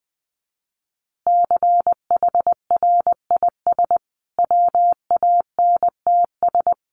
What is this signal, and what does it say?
Morse code, heard as a keyed tone, about 20 wpm, 700 Hz. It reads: CHRIS WANTS
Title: Additional Easy Phrases 20wpm